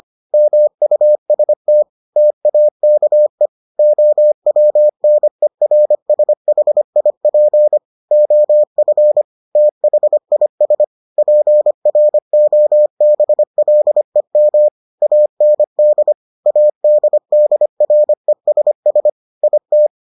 Morse code, heard as a keyed tone, about 25 wpm, 600 Hz. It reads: MUST TAKE OWNERSHIP OF THIS PROBLEM AND ADDRESS IT